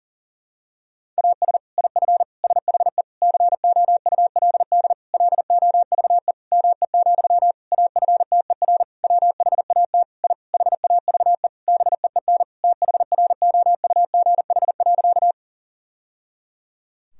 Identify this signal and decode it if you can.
Morse code, heard as a keyed tone, about 40 wpm, 700 Hz. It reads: AS IF SHE COULD LOVE ME, AFTER WHAT I HAVE BEEN THROUGH.
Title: Sentences from Top 1000 Words - Review Part 1 - 40wpm